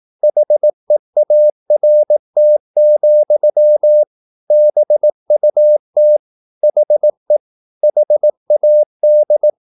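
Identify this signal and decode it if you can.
Morse code, heard as a keyed tone, about 18 wpm, 600 Hz. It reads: HEART, BUT HE HAD
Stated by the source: Morse practice at 18 wpm, English text, from CW med Gnister.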